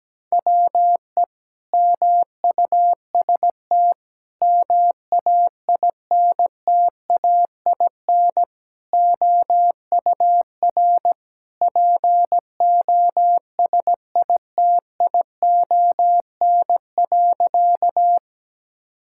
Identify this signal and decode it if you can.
Morse code, heard as a keyed tone, about 17 wpm, 700 Hz. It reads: WE MUST MAINTAIN OUR POSITION.
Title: Sentences from Top 1000 Words - Review Part 5 - 17wpm